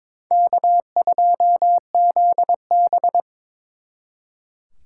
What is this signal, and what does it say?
Morse code, heard as a keyed tone, about 26 wpm, 700 Hz. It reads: K2ZB
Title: Speed Racing - USA 1x2 Call Signs 22wpm